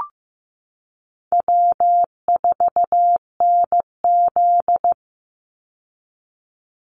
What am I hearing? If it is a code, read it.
Morse code, heard as a keyed tone, about 15 wpm, 700 Hz. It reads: W4NZ
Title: Call Signs Repeated 2x - 15wpm